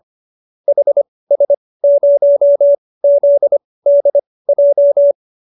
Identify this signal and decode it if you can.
Morse code, heard as a keyed tone, about 25 wpm, spaced as for 18 wpm, 590 Hz. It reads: HS0ZDJ
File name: qso216b 18